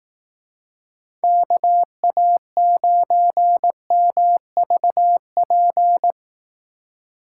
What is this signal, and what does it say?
Morse code, heard as a keyed tone, about 19 wpm, 700 Hz. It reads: KA9MVP
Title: Speed Racing - USA 2x3 Call Signs 15wpm